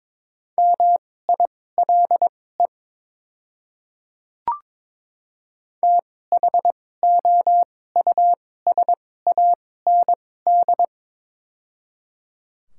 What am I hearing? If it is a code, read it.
Morse code, heard as a keyed tone, about 22 wpm, spaced as for 16 wpm, 700 Hz. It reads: MILE THOUSAND
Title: 2x Character Spacing - Top 500 Words - 22wpm